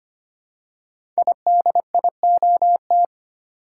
Morse code, heard as a keyed tone, about 25 wpm, 700 Hz. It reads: IDIOT
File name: 5 Letter Words - First 3 Letters Given - 25wpm